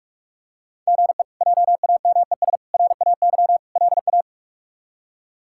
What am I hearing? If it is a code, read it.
Morse code, heard as a keyed tone, about 41 wpm, 700 Hz. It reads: GE JAMES RAY LA
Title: K1USN SST QSO Practice - 45wpm